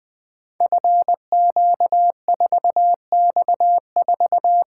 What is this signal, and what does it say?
Morse code, heard as a keyed tone, about 20 wpm, 700 Hz. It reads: FQ4X4
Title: Code Groups - 5 Letter:Numbers - 20wpm